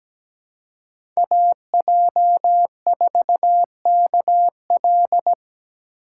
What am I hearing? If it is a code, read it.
Morse code, heard as a keyed tone, about 17 wpm, 700 Hz. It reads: AJ4KL
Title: USA Call Signs - 5 Characters - 3x Character Spacing - 17wpm